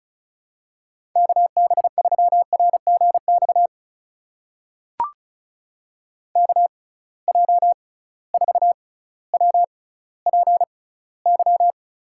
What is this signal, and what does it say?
Morse code, heard as a keyed tone, about 36 wpm, spaced as for 18 wpm, 700 Hz. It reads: KB3RGX KJ4WPY
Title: USA Call Signs - 6 Characters - 6x Character Spacing - 35wpm